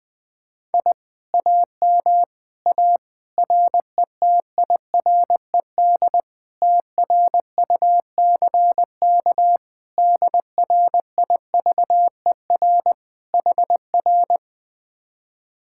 Morse code, heard as a keyed tone, about 20 wpm, 700 Hz. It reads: I AM A RETIRED TRUCK DRIVER HR
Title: Speed-Racing Common QSO Phrases - 20wpm